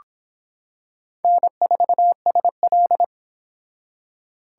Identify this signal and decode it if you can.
Morse code, heard as a keyed tone, about 25 wpm, 700 Hz. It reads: N4SL